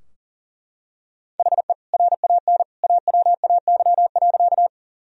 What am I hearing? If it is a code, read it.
Morse code, heard as a keyed tone, about 40 wpm, 700 Hz. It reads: HE RAN AWAY.
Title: Sentences from Top 1000 Words - Review Part 3 - 40wpm